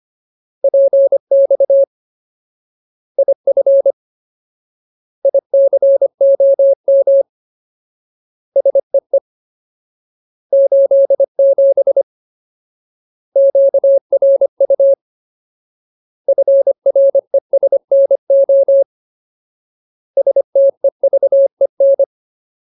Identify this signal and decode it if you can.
Morse code, heard as a keyed tone, about 25 wpm, 550 Hz. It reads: PX IF ICOM SEE 87 QRU FRESNO STEVEN
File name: Ultimate Word List / 25WPM Ultimate